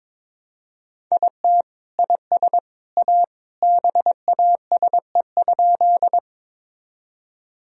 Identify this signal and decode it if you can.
Morse code, heard as a keyed tone, about 22 wpm, 700 Hz. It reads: IT IS A BASE?
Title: Sentences from Top 1000 Words - Review Part 1 - 22wpm